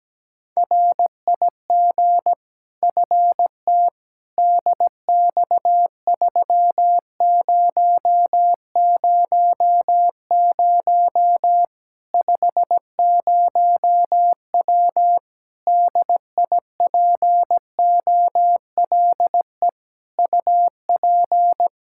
Morse code, heard as a keyed tone, about 17 wpm, 700 Hz. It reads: RIG FT DX3000 50W DIPOLE UP
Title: Practice QSO - Exchange 2 - Info - 17wpm